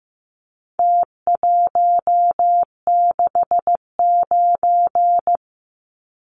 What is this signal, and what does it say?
Morse code, heard as a keyed tone, about 15 wpm, 700 Hz. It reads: T169
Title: Numbers - Contest Style - 4-Digit 15wpm